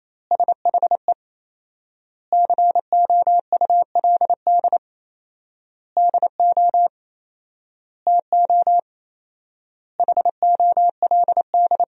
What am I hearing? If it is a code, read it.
Morse code, heard as a keyed tone, about 27 wpm, 700 Hz. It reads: SHE COULD DO TO HOLD
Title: 4x Word-Spacing - Sentences from Top 400 Words - 28wpm